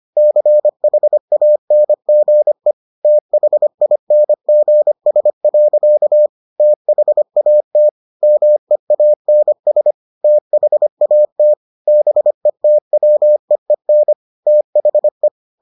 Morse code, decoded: CHANGE THINGS. THAT MEANS THAT BETWEEN THE

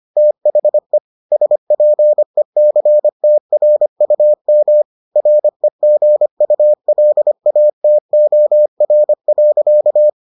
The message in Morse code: THE SPECTRUM REGULATOR.

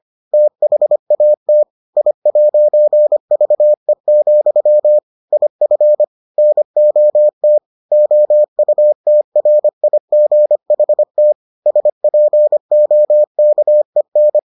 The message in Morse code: THAT I'VE, IF NOT OUTRIGHT SPOKEN

